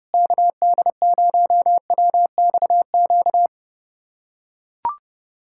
Morse code, heard as a keyed tone, about 30 wpm, 700 Hz. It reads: KD0WXQ
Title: USA Callsign DE Callsign - 6 Characters - 30wpm